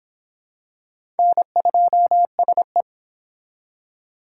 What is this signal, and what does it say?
Morse code, heard as a keyed tone, about 25 wpm, 700 Hz. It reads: N2SE